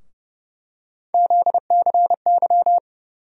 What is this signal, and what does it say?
Morse code, heard as a keyed tone, about 30 wpm, 700 Hz. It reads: ZCY
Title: USA Call Sign Postfixes - 3 Letters 30wpm